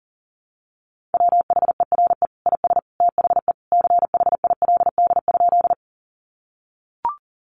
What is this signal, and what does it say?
Morse code, heard as a keyed tone, about 40 wpm, 700 Hz. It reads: WHERE IS THE CHILD?
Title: Sentences from Top 800 Words 40wpm